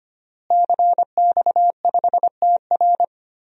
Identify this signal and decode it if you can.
Morse code, heard as a keyed tone, about 25 wpm, 700 Hz. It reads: CX5TR
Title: DXCC-100 Call Signs - 5 Characters - Repeated 3x - 25wpm